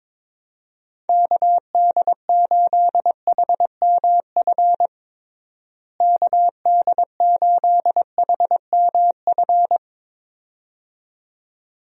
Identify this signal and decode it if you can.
Morse code, heard as a keyed tone, about 22 wpm, 700 Hz. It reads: KD8HMF KD8HMF
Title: USA Call Signs - 6 Characters - Repeated 2x - 22wpm